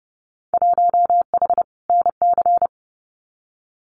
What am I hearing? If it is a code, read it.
Morse code, heard as a keyed tone, about 29 wpm, 700 Hz. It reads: 1H NC
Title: Winter Field Day Exchanges 30wpm